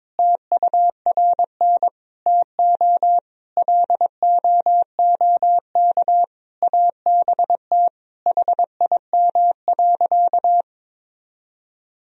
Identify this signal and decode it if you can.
Morse code, heard as a keyed tone, about 22 wpm, 700 Hz. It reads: TURN TO LOOK ABT HIM.
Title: Sentences from Top 200 Words and Abbreviations 22wpm